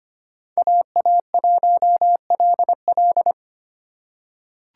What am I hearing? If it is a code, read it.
Morse code, heard as a keyed tone, about 25 wpm, 700 Hz. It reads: AA1LL
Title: USA Call Signs - 5 Characters - 8x Character Spacing - 25wpm